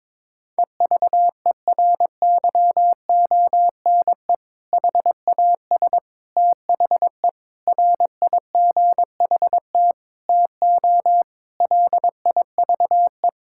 Morse code, heard as a keyed tone, about 22 wpm, 700 Hz. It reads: EVERYONE HAS THE RIGHT TO LIVE
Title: Sentences from Top 1000 Words - Review Part 5 - 22wpm